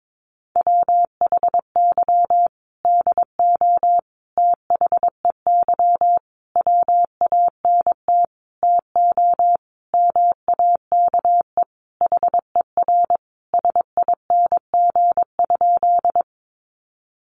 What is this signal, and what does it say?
Morse code, heard as a keyed tone, about 22 wpm, 700 Hz. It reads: WHY DO THEY WANT TO MAKE HER SING?
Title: Sentences from Top 400 Words 22wpm